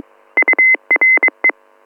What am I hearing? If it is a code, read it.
Morse code, heard as a keyed tone, about 23 wpm, 2050 Hz. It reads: URE